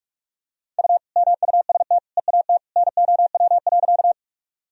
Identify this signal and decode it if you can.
Morse code, heard as a keyed tone, about 46 wpm, 700 Hz. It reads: U MUST EAT NOW.
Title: Sentences from Top 300 Words and Abbreviations 45wpm